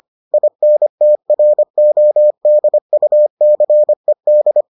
Morse code, heard as a keyed tone, about 25 wpm, 600 Hz. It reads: INTRODUCED